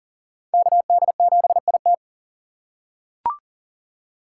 Morse code, heard as a keyed tone, about 40 wpm, 690 Hz. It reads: KD7IT
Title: USA Call Signs - 5 Characters - 3x Character Spacing - 40wpm